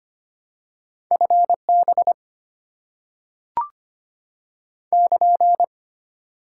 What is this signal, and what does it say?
Morse code, heard as a keyed tone, about 25 wpm, 700 Hz. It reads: FB <KN>
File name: Top 90 QSO Elements - 6x Character Spacing - 25wpm